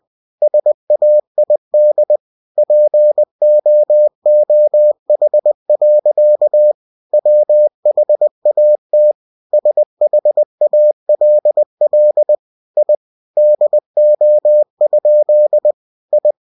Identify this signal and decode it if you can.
Morse code, heard as a keyed tone, about 20 wpm, 600 Hz. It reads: SAID POOH. WHAT SHALL I DO? I
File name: cw-books / winnie-the-pooh / winnie-20wpm-page-0050